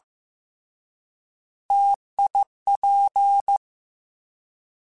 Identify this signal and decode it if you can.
Morse code, heard as a keyed tone, about 15 wpm, 800 Hz. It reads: TIP
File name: Difficult Combos / Difficult Combos-mp3s / T AOEI @13